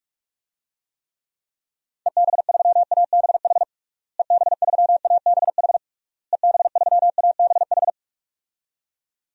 Morse code, heard as a keyed tone, about 46 wpm, 700 Hz. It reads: EB3ABH EB3ABH EB3ABH
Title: DXCC-100 Call Signs - 6 Characters - Repeated 3x - 45wpm